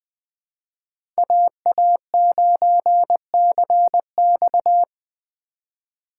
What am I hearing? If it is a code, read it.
Morse code, heard as a keyed tone, about 21 wpm, 700 Hz. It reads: AA9CX
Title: Speed Racing - USA 2x2 Call Signs 17wpm